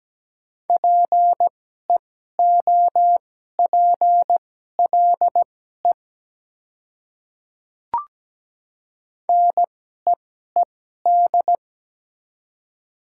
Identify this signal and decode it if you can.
Morse code, heard as a keyed tone, about 17 wpm, spaced as for 12 wpm, 700 Hz. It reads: PEOPLE NEED
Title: 2x Character Spacing - Top 200 Words - 17wpm